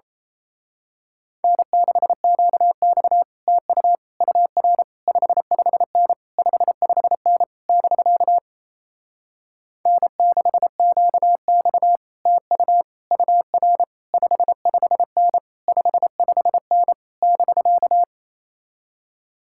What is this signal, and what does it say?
Morse code, heard as a keyed tone, about 30 wpm, 700 Hz. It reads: N6QX TU UR 55N 55N <BK> N6QX TU UR 55N 55N <BK>
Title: Speed-Racing Common QSO Phrases - 28wpm